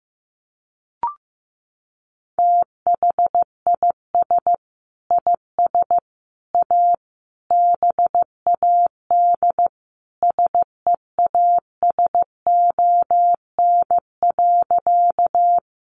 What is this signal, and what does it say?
Morse code, heard as a keyed tone, about 15 wpm, 700 Hz. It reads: THIS IS A BAD SEASON.
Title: Sentences from Top 1400 Words 15wpm